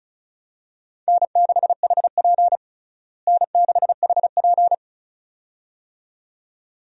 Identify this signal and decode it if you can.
Morse code, heard as a keyed tone, about 36 wpm, 700 Hz. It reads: N6HP N6HP
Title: USA Call Signs - 4 Characters - Repeated 2x - 35wpm